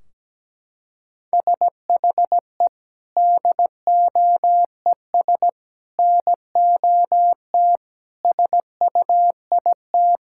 Morse code, heard as a keyed tone, about 17 wpm, 700 Hz. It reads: SHE DOES NOT SUIT
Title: Sentences from Top 1600 Words 17wpm